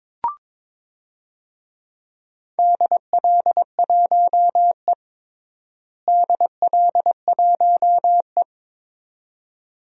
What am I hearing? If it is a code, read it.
Morse code, heard as a keyed tone, about 22 wpm, 700 Hz. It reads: DL1E DL1E
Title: DXCC-100 Call Signs - 4 Characters - Repeated 2x - 22wpm